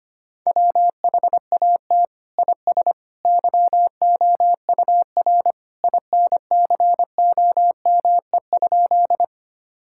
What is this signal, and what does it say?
Morse code, heard as a keyed tone, about 25 wpm, 700 Hz. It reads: WHAT IS YOUR INCOME?
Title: Sentences from Top 1000 Words - Review Part 5 - 25wpm